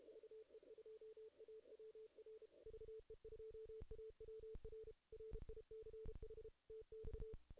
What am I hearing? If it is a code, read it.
Morse code, heard as a keyed tone, about 33 wpm, 440 Hz. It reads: VE2AWR VE2AWR RICH TX
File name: 2016 JAN CW NAQP / 2016 Jan naqp k5ot - 0-8